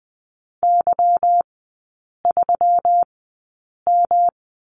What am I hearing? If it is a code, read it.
Morse code, heard as a keyed tone, about 19 wpm, spaced as for 8 wpm, 690 Hz. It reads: Y3M
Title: 145 Lesson 24e - Letter y - Three Characters 20wpm